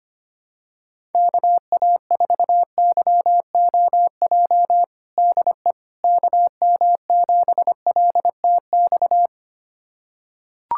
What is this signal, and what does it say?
Morse code, heard as a keyed tone, about 25 wpm, 700 Hz. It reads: KA4YOJ DE KM7LTX
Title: USA Callsign DE Callsign 25wpm